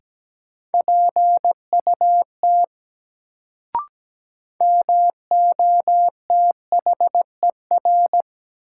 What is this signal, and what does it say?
Morse code, heard as a keyed tone, about 17 wpm, 700 Hz. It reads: PUT MOTHER
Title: Sentences from Top 200 Words - New Word Drill - 17wpm